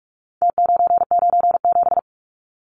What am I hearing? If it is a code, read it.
Morse code, heard as a keyed tone, about 44 wpm, 700 Hz. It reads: T997
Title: Numbers - Contest Style - 4-Digit 45wpm